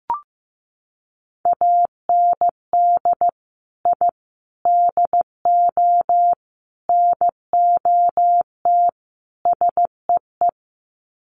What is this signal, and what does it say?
Morse code, heard as a keyed tone, about 15 wpm, 700 Hz. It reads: AND I DO NOT SEE
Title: Sets of 5 Words Encore2 15wpm